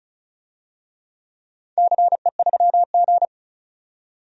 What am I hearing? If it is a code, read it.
Morse code, heard as a keyed tone, about 36 wpm, 700 Hz. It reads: CE3G
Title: DXCC-100 Call Signs - 4 Characters - 35wpm